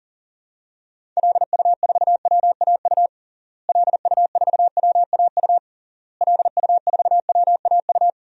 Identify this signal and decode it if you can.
Morse code, heard as a keyed tone, about 37 wpm, 700 Hz. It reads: LU4WAU LU4WAU LU4WAU
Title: DXCC-100 Call Signs - 6 Characters - Repeated 3x - 40wpm